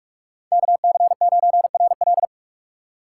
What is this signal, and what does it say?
Morse code, heard as a keyed tone, about 55 wpm, 700 Hz. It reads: KC9RL
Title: Speed Racing - USA 2x2 Call Signs 45wpm